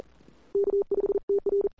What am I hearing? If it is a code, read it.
Morse code, heard as a keyed tone, about 38 wpm, 410 Hz. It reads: K5TR
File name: CWSprint / Feb2007 / 2007-feb-cw-sprint-k5tr-op-n5rz -00:00